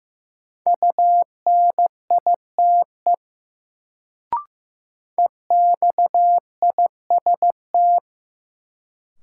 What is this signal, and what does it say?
Morse code, heard as a keyed tone, about 15 wpm, 700 Hz. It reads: UNITE EXIST
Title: Sentences from Top 1000 Words - Review Part 5 - New Word Drill - 15wpm